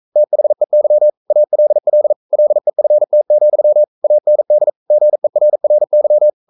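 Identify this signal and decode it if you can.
Morse code, heard as a keyed tone, about 42 wpm, 600 Hz. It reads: THEY ALL LEFT, AND GERRY
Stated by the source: Morse code practice at 41 wpm, from CW med Gnister.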